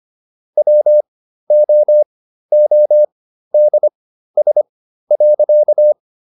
Morse code, heard as a keyed tone, about 25 wpm, spaced as for 13 wpm, 600 Hz. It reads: WOODS.